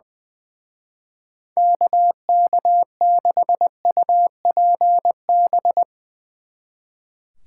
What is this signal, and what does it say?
Morse code, heard as a keyed tone, about 22 wpm, 700 Hz. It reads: KK6UPB
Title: Speed Racing - USA 2x3 Call Signs 20wpm